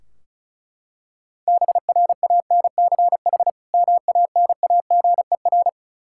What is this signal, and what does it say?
Morse code, heard as a keyed tone, about 36 wpm, 700 Hz. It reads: BRANCH MANAGER
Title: Practice QSO - Exchange 3 - More Info - 35wpm